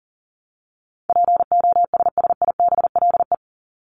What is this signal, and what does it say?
Morse code, heard as a keyed tone, about 40 wpm, 700 Hz. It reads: POSSIBLE
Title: Top 500 Words 40wpm